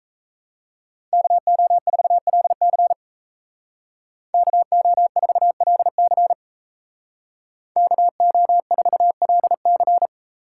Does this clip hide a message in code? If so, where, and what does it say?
Yes: Morse code from 1.1 to 10.1 s, heard as a keyed tone, about 35 wpm, 700 Hz. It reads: KO4LC KO4LC KO4LC